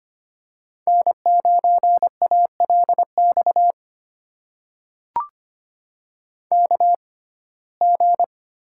Morse code, heard as a keyed tone, about 25 wpm, spaced as for 18 wpm, 700 Hz. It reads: N9ALX KG
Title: USA Call Signs - 5 Characters - 6x Character Spacing - 25wpm